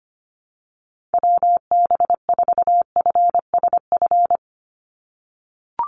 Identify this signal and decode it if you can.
Morse code, heard as a keyed tone, about 25 wpm, 700 Hz. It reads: WB4FSF
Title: Call Signs - 25wpm